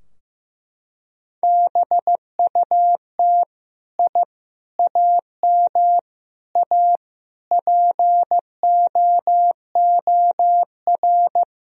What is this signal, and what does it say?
Morse code, heard as a keyed tone, about 16 wpm, 700 Hz. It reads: BUT I AM A POOR
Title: Sentences from Top 1000 Words - Review Part 3 - 15wpm